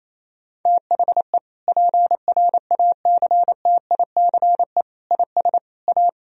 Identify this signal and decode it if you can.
Morse code, heard as a keyed tone, about 28 wpm, 700 Hz. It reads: THE PRACTICE IS A
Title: Sentences from Top 1000 Words - Review Part 4 - 28wpm